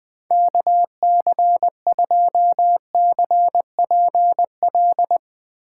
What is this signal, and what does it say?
Morse code, heard as a keyed tone, about 20 wpm, 700 Hz. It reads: KC2CPL
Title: USA Call Signs - 4x Character Spacing - 20wpm